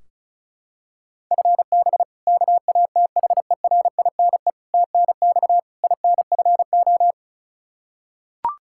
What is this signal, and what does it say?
Morse code, heard as a keyed tone, about 36 wpm, 700 Hz. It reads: FB KATHERINE TNX INFO
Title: Practice QSO - Exchange 3 - More Info - 35wpm